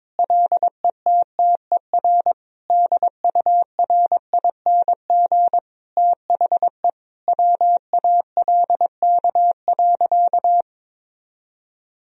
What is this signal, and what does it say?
Morse code, heard as a keyed tone, about 21 wpm, 700 Hz. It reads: LETTER DURING THE WALK.
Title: Sentences from Top 1000 Words - Review Part 2 - 22wpm